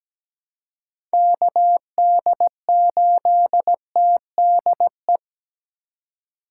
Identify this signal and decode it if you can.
Morse code, heard as a keyed tone, about 17 wpm, 700 Hz. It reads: KD8TDE
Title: USA Call Signs - 6 Characters - 2x Character Spacing - 17wpm